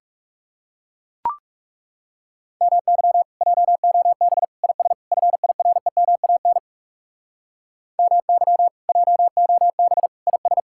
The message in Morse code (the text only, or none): MY JOB IS FIREMAN MY JOB IS